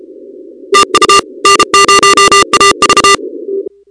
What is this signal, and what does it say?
Morse code, heard as a keyed tone, about 32 wpm, 1220 Hz, with noise. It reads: TU N0AV
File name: contests / 2019-cw-wpx-kt5j-k5pi- -2330 0-38